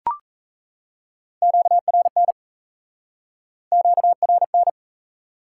Morse code, heard as a keyed tone, about 37 wpm, 700 Hz. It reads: QRN QRN